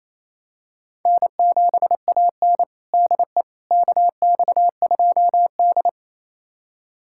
Morse code, heard as a keyed tone, about 28 wpm, 700 Hz. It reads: N7AN DE KX2D
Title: USA Callsign DE Callsign - 4 Characters - 28wpm